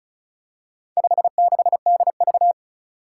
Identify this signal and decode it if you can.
Morse code, heard as a keyed tone, about 36 wpm, 700 Hz. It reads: 56DV
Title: Four Challenging Letter-Number Combinations 35wpm